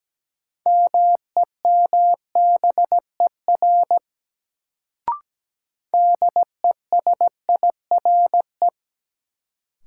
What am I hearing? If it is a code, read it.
Morse code, heard as a keyed tone, about 17 wpm, 700 Hz. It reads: MEMBER DESIRE
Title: Sentences from Top 1100 Words - New Word Drill - 17wpm